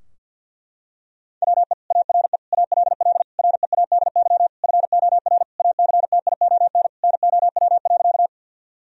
Morse code, heard as a keyed tone, about 50 wpm, 700 Hz. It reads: WE ARE ALL READY FOR ACTION NOW.
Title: Sentences from Top 1000 Words - Review Part 4 - 50wpm